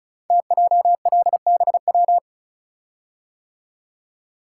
Morse code, heard as a keyed tone, about 36 wpm, 700 Hz. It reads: TJLBW
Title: Code Groups - 5 Letter:Numbers - 35wpm